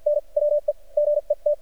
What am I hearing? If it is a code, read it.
Morse code, heard as a keyed tone, about 50 wpm, spaced as for 41 wpm, 590 Hz. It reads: N WE GET
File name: archived / CRSnet